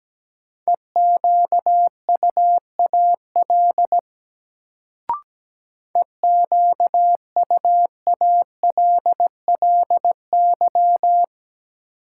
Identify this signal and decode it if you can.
Morse code, heard as a keyed tone, about 17 wpm, 700 Hz. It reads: EQUAL EQUALLY
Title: Sentences from Top 1400 Words - New Word Drill Alt - 17wpm